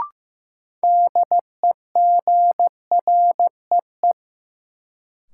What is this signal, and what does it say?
Morse code, heard as a keyed tone, about 15 wpm, 700 Hz. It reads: DEGREE
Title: Sentences from Top 1500 Words - New Word Drill - 15wpm